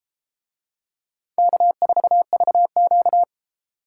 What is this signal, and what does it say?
Morse code, heard as a keyed tone, about 31 wpm, 700 Hz. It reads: K4VQ